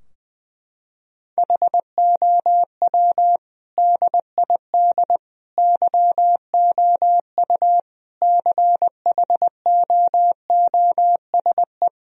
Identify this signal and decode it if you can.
Morse code, heard as a keyed tone, about 20 wpm, 700 Hz. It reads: HOW DID YOU CHOOSE
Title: Sentences from Top 1000 Words - Review Part 4 - 20wpm